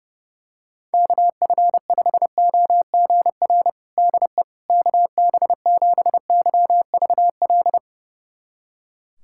Morse code, heard as a keyed tone, about 32 wpm, 700 Hz. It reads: KF5OGR DE KB7YVL
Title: USA Callsign DE Callsign - 6 Characters - 30wpm